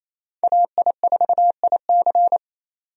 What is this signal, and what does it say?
Morse code, heard as a keyed tone, about 27 wpm, 700 Hz. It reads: AI4IC
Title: USA Call Signs - 5 Characters - 28wpm